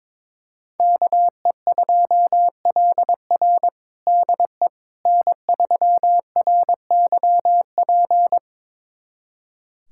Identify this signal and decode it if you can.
Morse code, heard as a keyed tone, about 22 wpm, 700 Hz. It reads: KE2LR DE N3RYP
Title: USA Callsign DE Callsign - 5 Characters - 22wpm